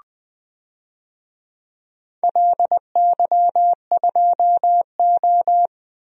Morse code, heard as a keyed tone, about 20 wpm, 700 Hz. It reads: LY2O
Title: DXCC-100 Call Signs - 4 Characters - 20wpm